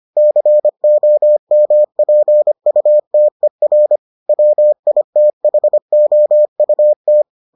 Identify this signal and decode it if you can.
Morse code, heard as a keyed tone, about 25 wpm, 600 Hz. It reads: COMPUTER WITHOUT